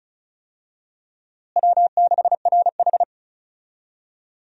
Morse code, heard as a keyed tone, about 34 wpm, 700 Hz. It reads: W6RH
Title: USA Call Signs - 4 Characters - Repeated 3x - 35wpm